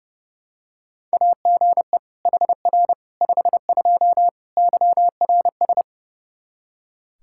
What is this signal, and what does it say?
Morse code, heard as a keyed tone, about 30 wpm, 700 Hz. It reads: AGE HR 52 YRS
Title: Practice QSO - Exchange 3 - More Info - 30wpm